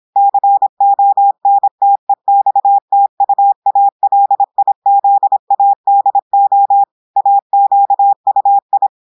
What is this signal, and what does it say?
Morse code, heard as a keyed tone, about 26 wpm, 800 Hz. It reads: CONTEXTUALIZADO AQUI
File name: audios / portugais / portugais texte en clair 1 vitesse 26 mots minutes